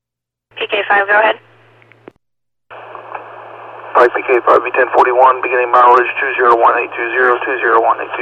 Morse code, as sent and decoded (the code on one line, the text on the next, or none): none